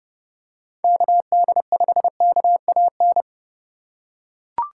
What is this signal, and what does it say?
Morse code, heard as a keyed tone, about 30 wpm, 700 Hz. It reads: KD5KAN